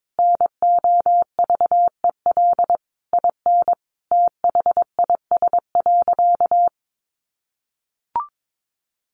Morse code, heard as a keyed tone, about 22 wpm, 700 Hz. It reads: NOVEL IN THIS.
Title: Sentences from Top 1300 Words 22wpm